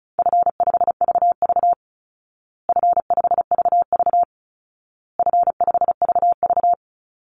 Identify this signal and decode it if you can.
Morse code, heard as a keyed tone, about 34 wpm, 700 Hz. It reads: F5VV F5VV F5VV
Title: Easy Call Signs Repeated 3x - 35wpm